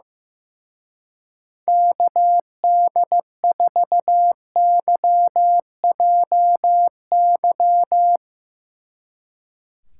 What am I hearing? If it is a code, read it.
Morse code, heard as a keyed tone, about 15 wpm, 700 Hz. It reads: KD4YJY
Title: Speed Racing - USA Call Signs 15wpm